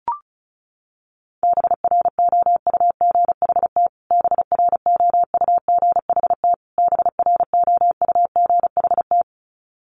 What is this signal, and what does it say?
Morse code, heard as a keyed tone, about 34 wpm, 700 Hz. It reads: BROUGHT BROUGHT BROUGHT
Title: Top 500 Words Repeated 35wpm